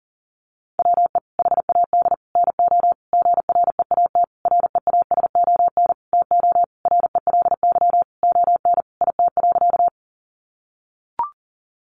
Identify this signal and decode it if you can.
Morse code, heard as a keyed tone, about 40 wpm, 700 Hz. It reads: WE HAD NO GREAT REASON TO RELY ON IT.
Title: Sentences from Top 1200 Words 40wpm